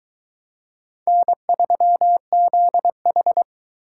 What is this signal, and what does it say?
Morse code, heard as a keyed tone, about 22 wpm, 700 Hz. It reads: N3ZH